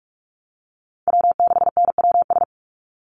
Morse code, heard as a keyed tone, about 44 wpm, 700 Hz. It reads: W6NWS
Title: Call Signs - 45wpm